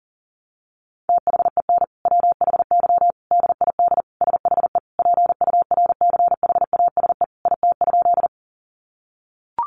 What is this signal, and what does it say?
Morse code, heard as a keyed tone, about 40 wpm, 700 Hz. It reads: THEN WHY DID SHE PURCHASE IT?
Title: Sentences from Top 800 Words 40wpm